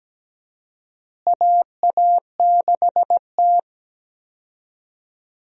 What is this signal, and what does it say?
Morse code, heard as a keyed tone, about 17 wpm, 700 Hz. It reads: AA6T
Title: USA Call Signs - 4 Characters - Repeated 2x - 17wpm